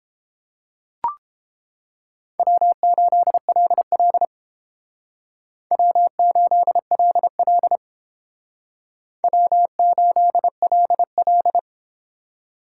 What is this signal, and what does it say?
Morse code, heard as a keyed tone, about 31 wpm, 700 Hz. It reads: W8LL W8LL W8LL